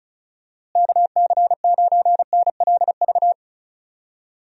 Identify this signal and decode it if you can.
Morse code, heard as a keyed tone, about 34 wpm, 700 Hz. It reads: KC9NLV
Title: USA Call Signs - 6 Characters - Repeated 3x - 35wpm